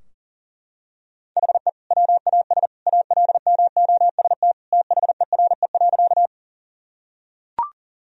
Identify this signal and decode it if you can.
Morse code, heard as a keyed tone, about 40 wpm, 700 Hz. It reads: HE WAS ALMOST THERE.
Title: Sentences from Top 1000 Words - Review Part 4 - 40wpm